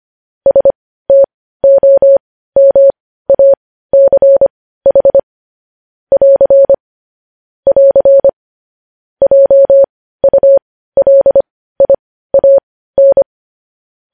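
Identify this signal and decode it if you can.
Morse code, heard as a keyed tone, about 25 wpm, spaced as for 15 wpm, 550 Hz. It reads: STOMACH <AR> <AR> JULIAN
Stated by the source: Morse news headlines practice at 15 wpm.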